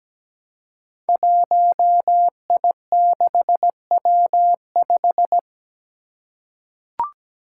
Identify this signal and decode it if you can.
Morse code, heard as a keyed tone, about 17 wpm, 700 Hz. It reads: 1I6W5
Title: Five Challenging Letter-Number Combinations 17wpm